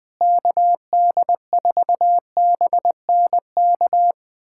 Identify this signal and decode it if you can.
Morse code, heard as a keyed tone, about 21 wpm, 700 Hz. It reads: KD4BNK
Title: USA Callsign DE Callsign - 6 Characters - 20wpm